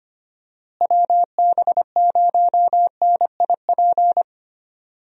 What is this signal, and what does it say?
Morse code, heard as a keyed tone, about 25 wpm, 700 Hz. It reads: WB0NIP
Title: USA Call Signs - 6 Characters - Repeated 2x - 25wpm